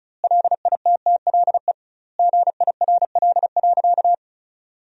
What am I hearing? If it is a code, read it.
Morse code, heard as a keyed tone, about 36 wpm, 700 Hz. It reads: LITTLE GIRL.
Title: 2x Word-Spacing - Sentences from Top 400 Words - 35wpm